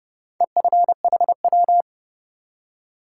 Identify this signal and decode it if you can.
Morse code, heard as a keyed tone, about 30 wpm, 700 Hz. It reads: EFHW
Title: Top 180 QSO Elements 30wpm